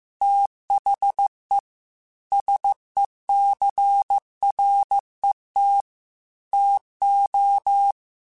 Morse code, heard as a keyed tone, about 15 wpm, 800 Hz. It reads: THE SECRET TO
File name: Technical Subjects / Technical Subjects-mp3s / Sw Pwr Sup @13